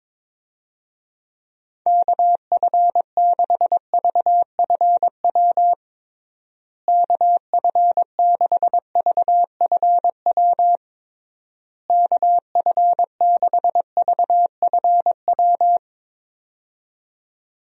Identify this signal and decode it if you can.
Morse code, heard as a keyed tone, about 21 wpm, 700 Hz. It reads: KF6VFW KF6VFW KF6VFW
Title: USA Call Signs - 6 Characters - Repeated 3x - 22wpm